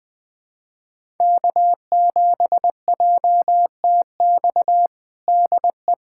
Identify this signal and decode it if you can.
Morse code, heard as a keyed tone, about 21 wpm, 700 Hz. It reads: K7JTX DE
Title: USA Callsign DE Callsign - 5 Characters - 20wpm